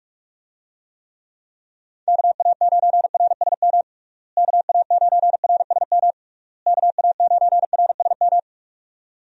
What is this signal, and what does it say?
Morse code, heard as a keyed tone, about 41 wpm, 700 Hz. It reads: KA9RSM KA9RSM KA9RSM
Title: USA Call Signs - 6 Characters - Repeated 3x - 45wpm